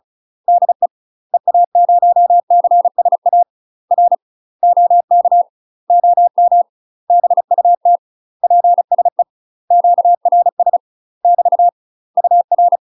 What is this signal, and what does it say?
Morse code, heard as a keyed tone, about 36 wpm, 700 Hz. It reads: DE EA0CSA R OK OM BUT PSE QRS <BT> UR
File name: UK MASTER QSOs TRAINING / 35WPM / UK MASTER QSOs 0009